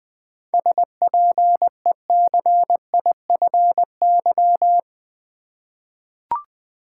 Morse code, heard as a keyed tone, about 20 wpm, 700 Hz. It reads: SPECIFY
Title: Seven Letter Words - 20wpm